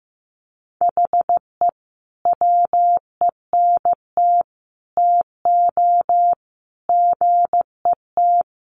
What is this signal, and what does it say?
Morse code, heard as a keyed tone, about 15 wpm, 700 Hz. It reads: HE WENT TO GET
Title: Sentences from Top 1500 Words 15wpm